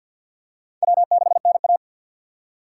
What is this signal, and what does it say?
Morse code, heard as a keyed tone, about 51 wpm, 700 Hz. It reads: W6NA
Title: USA Call Signs - 4 Characters - 6x Character Spacing - 50wpm